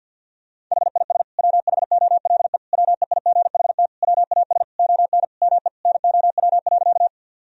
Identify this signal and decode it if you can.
Morse code, heard as a keyed tone, about 50 wpm, 700 Hz. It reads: HIS WHOLE WEIGHT WAS ON ME NOW.
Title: Sentences from Top 1000 Words - Review Part 3 - 50wpm